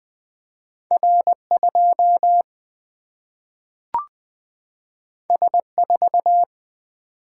Morse code, heard as a keyed tone, about 20 wpm, 700 Hz. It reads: R2 S4
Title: Single-Letter Single-Number 20wpm